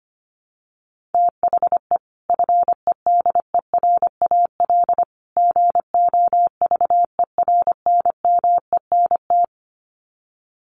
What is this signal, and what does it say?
Morse code, heard as a keyed tone, about 25 wpm, 700 Hz. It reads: THE FEDERAL GOVERNMENT
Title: Sets of 3 Words 25wpm